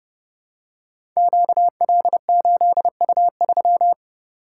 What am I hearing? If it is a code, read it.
Morse code, heard as a keyed tone, about 30 wpm, 700 Hz. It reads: QL8U3
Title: Code Groups - 5 Letter:Numbers - 30wpm